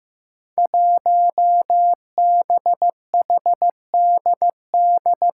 Morse code, heard as a keyed tone, about 15 wpm, 700 Hz. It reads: 1BHDD
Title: Five Challenging Letter-Number Combinations 15wpm